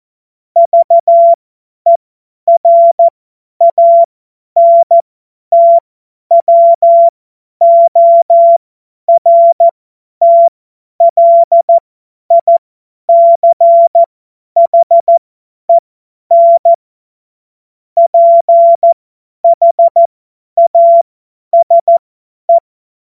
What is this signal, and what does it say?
Morse code, heard as a keyed tone, about 14 wpm, spaced as for 10 wpm, 680 Hz. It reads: VERANTWORTLICHEN PHASE
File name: Morsekurs / Witze / Witz 09. Dezember 2025 50 F70